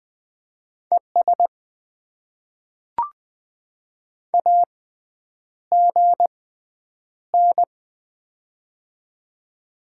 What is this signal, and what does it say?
Morse code, heard as a keyed tone, about 20 wpm, spaced as for 9 wpm, 700 Hz. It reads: ES AGN
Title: Top 20 QSO Elements - 6x Character Spacing - 20wpm